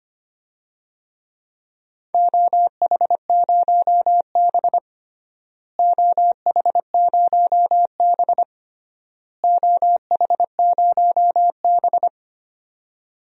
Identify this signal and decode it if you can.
Morse code, heard as a keyed tone, about 25 wpm, 700 Hz. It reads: OH0B OH0B OH0B
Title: DXCC-100 Call Signs - 4 Characters - Repeated 3x - 25wpm